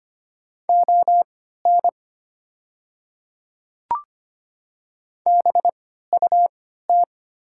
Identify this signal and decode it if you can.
Morse code, heard as a keyed tone, about 25 wpm, spaced as for 14 wpm, 690 Hz. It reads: ON BUT
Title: 3x Character Spacing - Top 100 Words - 25wpm